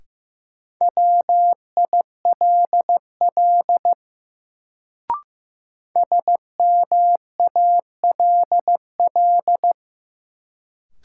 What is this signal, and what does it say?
Morse code, heard as a keyed tone, about 16 wpm, 700 Hz. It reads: WILL SMALL
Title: Sets of 2 Words - New Word Drill - 15wpm